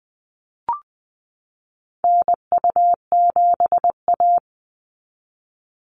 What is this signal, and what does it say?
Morse code, heard as a keyed tone, about 20 wpm, 700 Hz. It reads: NU7A
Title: NA SOTA Activator Callsigns 20wpm